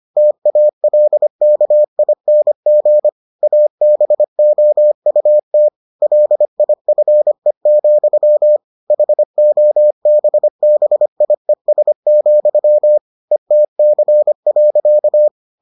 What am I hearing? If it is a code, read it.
Morse code, heard as a keyed tone, about 25 wpm, 600 Hz. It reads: TALKING ABOUT LIFE, HOBBIES, ETC.